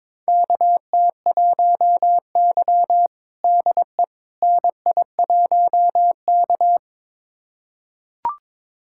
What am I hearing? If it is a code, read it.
Morse code, heard as a keyed tone, about 22 wpm, 700 Hz. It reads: KT1Y DE NI1K
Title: USA Callsign DE Callsign - 4 Characters - 22wpm